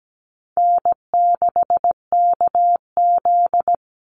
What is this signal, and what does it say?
Morse code, heard as a keyed tone, about 17 wpm, 700 Hz. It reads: N6KZ
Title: NA SOTA Chaser Callsigns 17wpm